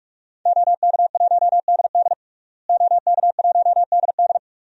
Morse code, heard as a keyed tone, about 46 wpm, 700 Hz. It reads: OK1DD OK1DD
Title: DXCC-100 Call Signs - 5 Characters - Repeated 2x - 45wpm